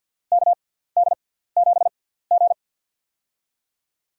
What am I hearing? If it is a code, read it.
Morse code, heard as a keyed tone, about 53 wpm, spaced as for 17 wpm, 700 Hz. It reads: KD7G
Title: USA Call Signs - 6x Character Spacing - 50wpm